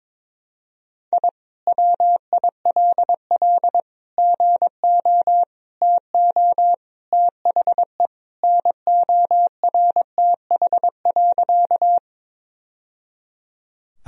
Morse code, heard as a keyed tone, about 22 wpm, 700 Hz. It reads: I WILL GO TO THE NORTH.
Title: Sentences from Top 1000 Words - Review Part 1 - 22wpm